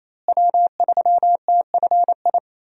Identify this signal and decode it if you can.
Morse code, heard as a keyed tone, about 28 wpm, 700 Hz. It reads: W3TFI
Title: USA Call Signs - 5 Characters - Repeated 3x - 28wpm